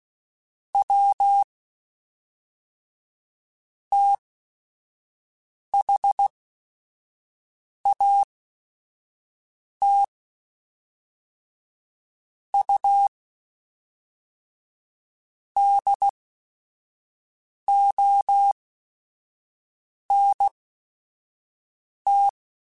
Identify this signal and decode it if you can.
Morse code, heard as a keyed tone, about 16 wpm, spaced as for 5 wpm, 800 Hz. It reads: W THAT U DONT
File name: QSOs / QSOs-mp3s / WD0BGP 2@5